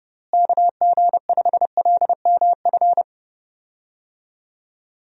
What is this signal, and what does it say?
Morse code, heard as a keyed tone, about 30 wpm, 700 Hz. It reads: KG5LMF
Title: USA Call Signs - 6 Characters - Repeated 2x - 30wpm